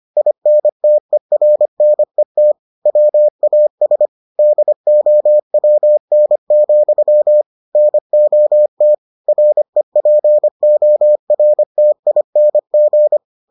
Morse code, heard as a keyed tone, about 25 wpm, 600 Hz. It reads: INTERNET WAS DOWN, NOT REPORTING